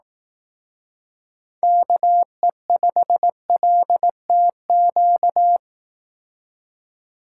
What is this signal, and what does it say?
Morse code, heard as a keyed tone, about 18 wpm, 700 Hz. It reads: KE5LTQ
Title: Speed Racing - USA 2x3 Call Signs 15wpm